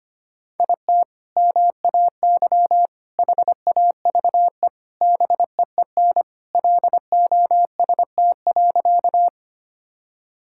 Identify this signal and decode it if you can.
Morse code, heard as a keyed tone, about 25 wpm, 700 Hz. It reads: IT MAY HAVE BEEN LOST.
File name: practice / 25wpm / Sentences with 3-Word Phrases - 25wpm